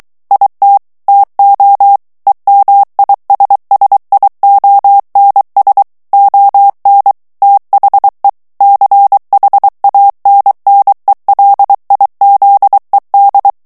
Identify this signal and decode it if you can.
Morse code, heard as a keyed tone, about 23 wpm, 800 Hz. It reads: IT TO EMISSIONS ON THE CHANNELIZED
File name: ANGLAIS texte clair 3 vitesse 22 mots minute